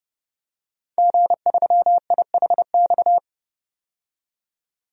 Morse code, heard as a keyed tone, about 32 wpm, 700 Hz. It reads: G3IHX
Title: DXCC-100 Call Signs - 5 Characters - 30wpm